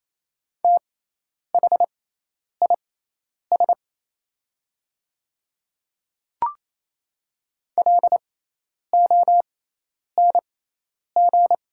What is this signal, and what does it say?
Morse code, heard as a keyed tone, about 28 wpm, spaced as for 10 wpm, 700 Hz. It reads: THIS LONG
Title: 6x Character Spacing - Top 100 Words - 28wpm